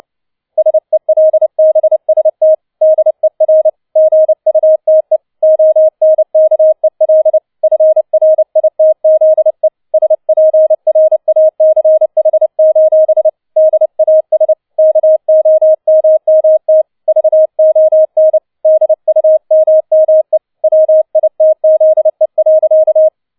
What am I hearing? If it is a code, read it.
Morse code, heard as a keyed tone, about 29 wpm, 630 Hz. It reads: SELBST DER GUTE ONKEL FRITZE SPRACH: DAS KOMMT VON DUMME WITZE.
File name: 007 Letzter Streich 30wpm